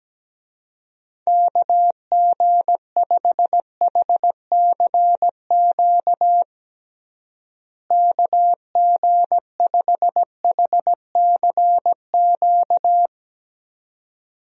Morse code, heard as a keyed tone, about 17 wpm, 700 Hz. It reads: KG5HCQ KG5HCQ
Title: USA Call Signs - 6 Characters - Repeated 2x - 17wpm